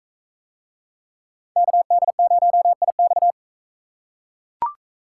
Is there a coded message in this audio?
Yes: Morse code from 1.6 to 3.3 s, heard as a keyed tone, about 39 wpm, 700 Hz. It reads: KD0IX